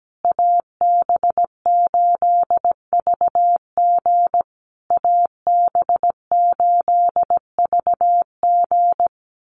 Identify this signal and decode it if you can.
Morse code, heard as a keyed tone, about 17 wpm, 700 Hz. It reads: AB8VG AB8VG
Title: USA Call Signs Repeated 3x 17wpm